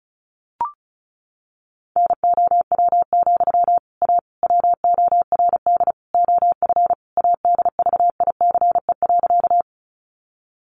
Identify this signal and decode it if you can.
Morse code, heard as a keyed tone, about 34 wpm, 700 Hz. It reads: NOW, A WORD OF ADVICE.
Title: Sentences from Top 1100 Words 35wpm